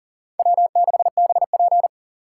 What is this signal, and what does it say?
Morse code, heard as a keyed tone, about 40 wpm, 700 Hz. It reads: W6BP
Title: USA Call Signs - 4 Characters - 3x Character Spacing - 40wpm